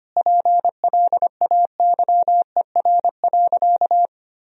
Morse code, heard as a keyed tone, about 25 wpm, 700 Hz. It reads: PLAYER.
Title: Sentences from Top 1000 Words - Review Part 4 - 25wpm